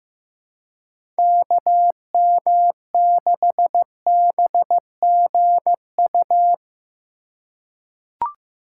Morse code, heard as a keyed tone, about 15 wpm, 700 Hz. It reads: KM6BGU
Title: USA Call Signs - 6 Characters - 6x Character Spacing - 15wpm